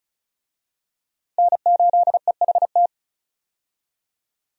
Morse code, heard as a keyed tone, about 36 wpm, 700 Hz. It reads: N8EHT
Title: USA Call Signs - 5 Characters - Repeated 3x - 35wpm